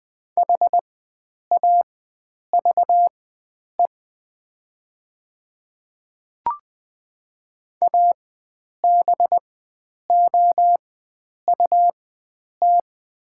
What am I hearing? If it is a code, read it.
Morse code, heard as a keyed tone, about 20 wpm, spaced as for 9 wpm, 700 Hz. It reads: HAVE ABOUT
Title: 4x Character Spacing - Top 100 Words - 20wpm